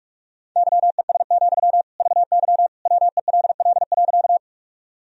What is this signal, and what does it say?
Morse code, heard as a keyed tone, about 43 wpm, 700 Hz. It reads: YES, VY WELL.
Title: Sentences from Top 500 Words and Abbreviations 45wpm